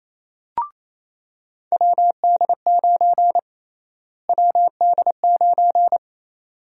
Morse code, heard as a keyed tone, about 28 wpm, 700 Hz. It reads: WD9 WD9
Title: USA Call Sign Prefixes - Repeated 2x - 28wpm